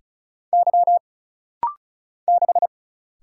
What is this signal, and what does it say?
Morse code, heard as a keyed tone, about 36 wpm, 700 Hz. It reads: Y 6
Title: Single Letter-Number - Rapid-Fire 35wpm